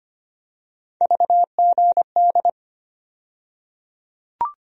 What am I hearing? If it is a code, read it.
Morse code, heard as a keyed tone, about 25 wpm, 700 Hz. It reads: VGD
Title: USA Call Sign Postfixes - 3 Letters 25wpm